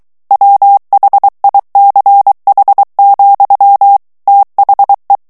Morse code, heard as a keyed tone, about 23 wpm, 800 Hz. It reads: WHICH, THE
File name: ANGLAIS texte clair 3 vitesse 22 mots minute